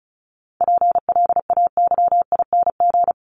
Morse code, heard as a keyed tone, about 34 wpm, 700 Hz. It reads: PLAYING
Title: ING Suffix - Top 100 Words with Base Words 35wpm